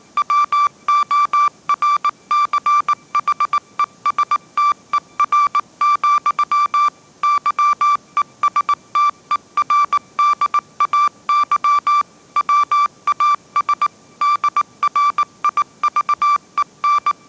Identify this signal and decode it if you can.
Morse code, heard as a keyed tone, about 19 wpm, 1170 Hz. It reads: WORCHESTER, YESTERDAY WAS DRIVEN
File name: 20wpm